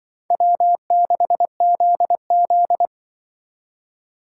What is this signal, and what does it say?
Morse code, heard as a keyed tone, about 24 wpm, 700 Hz. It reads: W6ZZ
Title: Speed Racing - USA 1x2 Call Signs 20wpm